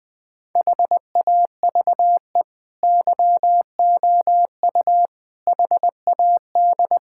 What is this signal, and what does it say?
Morse code, heard as a keyed tone, about 21 wpm, 700 Hz. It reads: HAVE YOU HAD
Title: Sentences from Top 1000 Words - Review Part 6 - 20wpm